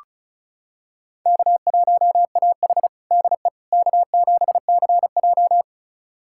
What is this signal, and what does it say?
Morse code, heard as a keyed tone, about 36 wpm, 700 Hz. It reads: K1AH DE K7CJ
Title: USA Callsign DE Callsign - 4 Characters - 35wpm